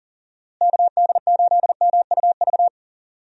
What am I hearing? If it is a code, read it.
Morse code, heard as a keyed tone, about 40 wpm, 700 Hz. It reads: KD8MUV
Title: USA Call Signs - 6 Characters - Repeated 3x - 40wpm